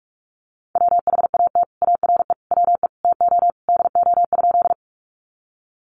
Morse code, heard as a keyed tone, about 44 wpm, 700 Hz. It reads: WHAT ARE WE TO DO?
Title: Sentences from Top 100 Words 45wpm